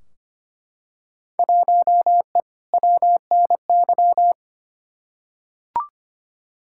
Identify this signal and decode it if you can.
Morse code, heard as a keyed tone, about 25 wpm, 700 Hz. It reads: 1E WNY
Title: Field Day Exchanges - 25wpm